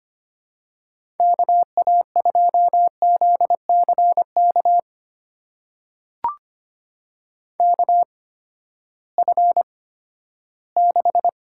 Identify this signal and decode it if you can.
Morse code, heard as a keyed tone, about 25 wpm, spaced as for 14 wpm, 700 Hz. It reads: KA2ZCK KF6
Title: USA Call Signs - 6 Characters - 8x Character Spacing - 25wpm